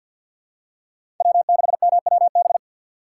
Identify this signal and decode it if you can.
Morse code, heard as a keyed tone, about 50 wpm, 700 Hz. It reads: W6MWB
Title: USA Call Signs - 5 Characters - 4x Character Spacing - 50wpm